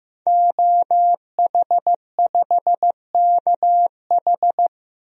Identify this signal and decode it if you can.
Morse code, heard as a keyed tone, about 15 wpm, 700 Hz. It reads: OH5KH
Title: DXCC-100 Call Signs - 5 Characters - Repeated 2x - 15wpm